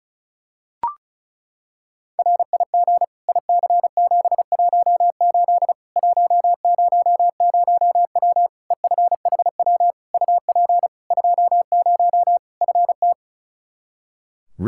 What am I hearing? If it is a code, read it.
Morse code, heard as a keyed tone, about 34 wpm, 700 Hz. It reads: RIG IC718 100W EFHW UP 20 FT
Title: Practice QSO - Exchange 2 - Info - 35wpm